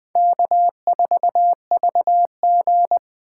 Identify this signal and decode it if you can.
Morse code, heard as a keyed tone, about 21 wpm, 700 Hz. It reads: K4VG
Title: Speed Racing - USA Call Signs 17wpm